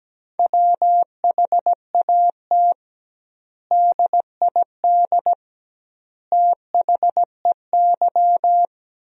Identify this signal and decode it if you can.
Morse code, heard as a keyed tone, about 17 wpm, 700 Hz. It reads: WHAT DID THEY
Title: 2x Word-Spacing - Sentences from Top 100 Words - 17wpm